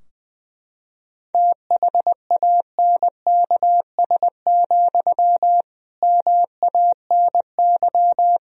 Morse code, heard as a keyed tone, about 20 wpm, 700 Hz. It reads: THANKS, MANY
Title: Sentences from Top 1000 Words - Review Part 3 - 20wpm